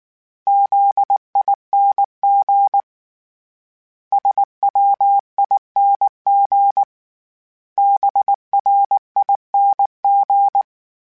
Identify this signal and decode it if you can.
Morse code, heard as a keyed tone, about 19 wpm, 800 Hz. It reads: ZING SWING BRING
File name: Special INGs / Special INGs-mp3s / Sngl Ing @18